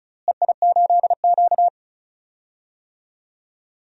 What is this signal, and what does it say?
Morse code, heard as a keyed tone, about 36 wpm, 700 Hz. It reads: EI8Q
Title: DXCC-100 Call Signs - 4 Characters - Repeated 2x - 35wpm